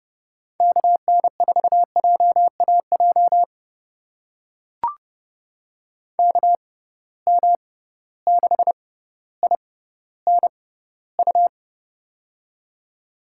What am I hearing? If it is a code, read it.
Morse code, heard as a keyed tone, about 30 wpm, spaced as for 15 wpm, 700 Hz. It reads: KN4JAJ KM6INU
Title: USA Call Signs - 6x Character Spacing - 30wpm